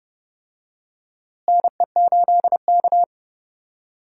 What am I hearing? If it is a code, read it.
Morse code, heard as a keyed tone, about 30 wpm, 700 Hz. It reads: NE8K
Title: USA Call Signs - 4 Characters - Repeated 3x - 30wpm